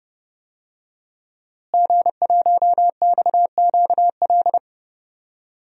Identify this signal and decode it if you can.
Morse code, heard as a keyed tone, about 30 wpm, 700 Hz. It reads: G1XQL
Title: DXCC-100 Call Signs - 5 Characters - Repeated 3x - 30wpm